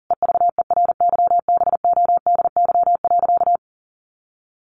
Morse code, heard as a keyed tone, about 40 wpm, 700 Hz. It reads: EVERYBODY.
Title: Sentences from Top 1400 Words 40wpm